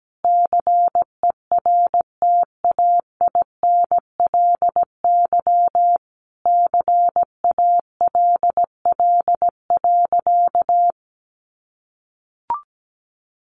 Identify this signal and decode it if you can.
Morse code, heard as a keyed tone, about 17 wpm, 700 Hz. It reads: CERTAINLY CALL.
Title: Sentences from Top 900 Words 17wpm